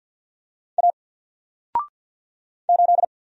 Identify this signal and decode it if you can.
Morse code, heard as a keyed tone, about 48 wpm, 700 Hz. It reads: A 8
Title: Single Character (Letter:Number:Punctuation:Prosign) Alt 50wpm